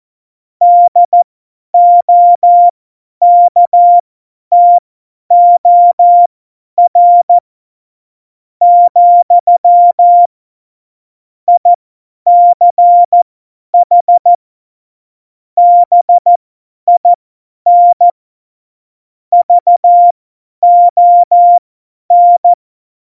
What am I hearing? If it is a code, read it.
Morse code, heard as a keyed tone, about 14 wpm, spaced as for 9 wpm, 700 Hz. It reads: DOKTOR, ICH BIN VON
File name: Morsekurs / Witze / Witz 06 March 2018, 50 F70